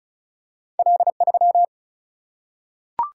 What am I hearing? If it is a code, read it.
Morse code, heard as a keyed tone, about 36 wpm, 700 Hz. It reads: L3
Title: Single-Letter Single-Number 35wpm